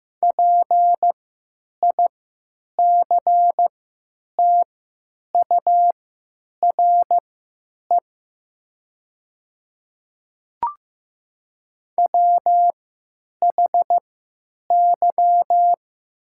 Morse code, heard as a keyed tone, about 15 wpm, spaced as for 9 wpm, 700 Hz. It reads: PICTURE WHY
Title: 3x Character Spacing - Top 500 Words - 15wpm